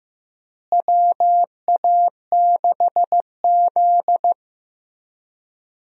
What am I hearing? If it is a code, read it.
Morse code, heard as a keyed tone, about 15 wpm, 700 Hz. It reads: WA6Z
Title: USA Call Signs - 4 Characters - Repeated 3x - 15wpm